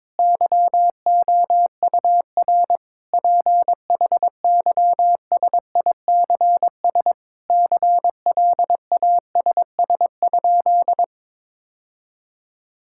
Morse code, heard as a keyed tone, about 22 wpm, 700 Hz. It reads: YOUR PHYSICS CLASS?
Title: Sentences from Top 1000 Words - Review Part 5 - 22wpm